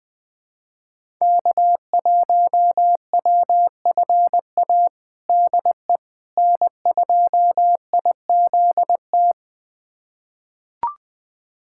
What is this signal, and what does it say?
Morse code, heard as a keyed tone, about 20 wpm, 700 Hz. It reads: K1WFA DE N2IZT
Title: USA Callsign DE Callsign - 5 Characters - 20wpm